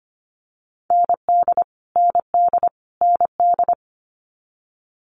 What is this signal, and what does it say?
Morse code, heard as a keyed tone, about 25 wpm, 700 Hz. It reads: ND ND ND
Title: Multi-letter Phonemes Repeated 3x 25wpm